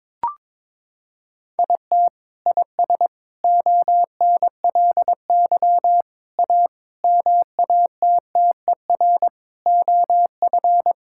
Morse code, heard as a keyed tone, about 22 wpm, 700 Hz. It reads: IT IS ONLY A MATTER OF
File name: practice / 22wpm / Sets of 8 Words - 22wpm